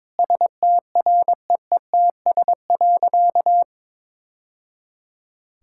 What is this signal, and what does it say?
Morse code, heard as a keyed tone, about 23 wpm, 700 Hz. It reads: STREETS.
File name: practice / 22wpm / Easy Sentences with Hints - 22wpm